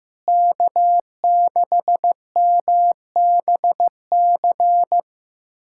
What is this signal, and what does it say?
Morse code, heard as a keyed tone, about 15 wpm, 700 Hz. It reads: K6MBC
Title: USA Callsign DE Callsign - 5 Characters - 15wpm